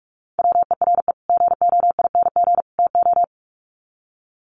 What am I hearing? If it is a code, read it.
Morse code, heard as a keyed tone, about 44 wpm, 700 Hz. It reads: WERE GOING TO
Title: Sets of 3 Words 45wpm